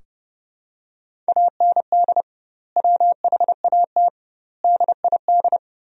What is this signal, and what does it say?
Morse code, heard as a keyed tone, about 30 wpm, 700 Hz. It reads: AND WHAT DID
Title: 2x Word-Spacing - Sentences from Top 300 Words - 30wpm